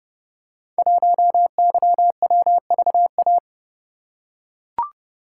Morse code, heard as a keyed tone, about 30 wpm, 700 Hz. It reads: 1YWVA
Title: Code Groups - 5 Letter:Numbers - 30wpm